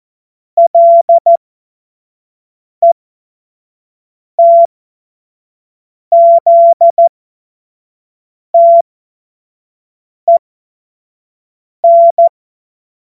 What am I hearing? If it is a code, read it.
Morse code, heard as a keyed tone, about 14 wpm, spaced as for 5 wpm, 680 Hz. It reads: LETZTEN
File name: Morsekurs / Witze / Witz 08. November 2022 25 F70